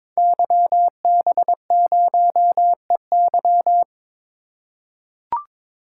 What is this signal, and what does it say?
Morse code, heard as a keyed tone, about 22 wpm, 700 Hz. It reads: YB0EY
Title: Code Groups - 5 Letter:Numbers - 22wpm